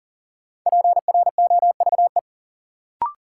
Morse code, heard as a keyed tone, about 40 wpm, 700 Hz. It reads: PROVE
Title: Sentences from Top 1600 Words - New Word Drill Alt - 40wpm